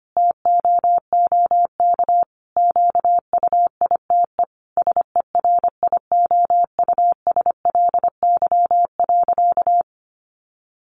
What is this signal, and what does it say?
Morse code, heard as a keyed tone, about 25 wpm, 700 Hz. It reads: TOOK QUITE SERIOUSLY.
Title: Sentences from Top 1200 Words 25wpm